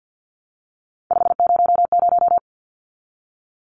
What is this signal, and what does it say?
Morse code, heard as a keyed tone, about 50 wpm, 700 Hz. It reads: 500
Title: Numbers - 3-Digit 50wpm